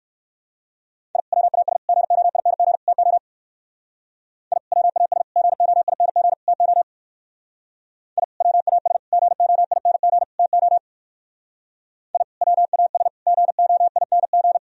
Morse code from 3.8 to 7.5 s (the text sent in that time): I WAS GOING TO